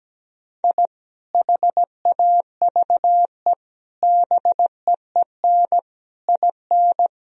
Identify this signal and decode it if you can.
Morse code, heard as a keyed tone, about 17 wpm, 700 Hz. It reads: I HAVE BEEN IN
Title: Sentences from Top 1000 Words - Review Part 3 - 17wpm